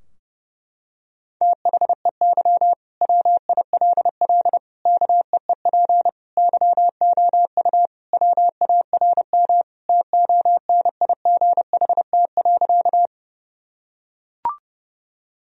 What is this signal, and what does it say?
Morse code, heard as a keyed tone, about 30 wpm, 700 Hz. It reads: THEY WILL KEEP YOU WARM TONIGHT.
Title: Sentences from Top 1000 Words - Review Part 3 - 30wpm